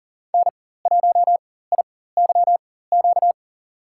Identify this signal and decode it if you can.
Morse code, heard as a keyed tone, about 40 wpm, spaced as for 19 wpm, 700 Hz. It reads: N1IYQ
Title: USA Call Signs - 5 Characters - 4x Character Spacing - 40wpm